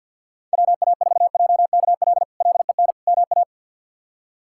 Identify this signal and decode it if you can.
Morse code, heard as a keyed tone, about 50 wpm, 700 Hz. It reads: WA4JKL LEN MA
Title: K1USN SST QSO Practice - 50wpm